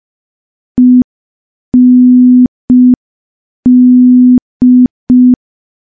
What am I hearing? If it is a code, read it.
Morse code, heard as a keyed tone, about 5 wpm, 250 Hz. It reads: END
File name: morse / morse-training-05WPM